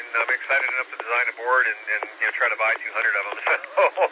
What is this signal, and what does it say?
Emergency Alert System (SAME) tones, heard as harsh buzzing bursts, with speech. It is heard from 0.2 to 3.2 s.